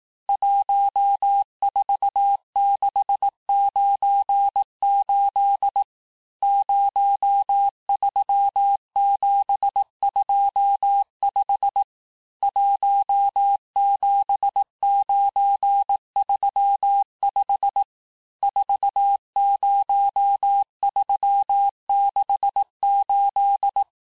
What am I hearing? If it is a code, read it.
Morse code, heard as a keyed tone, about 18 wpm, 800 Hz. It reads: 14698 03725 17935 40368
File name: cours-du-lundi files / cours-1594--teste-chiffre-vitesse-17mpm